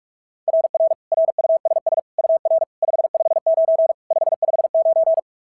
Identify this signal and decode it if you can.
Morse code, heard as a keyed tone, about 44 wpm, 690 Hz. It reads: RR RUSS UR 559 559
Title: SOTA QSO Practice 45wpm